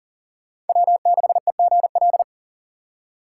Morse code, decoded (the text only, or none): W6EGL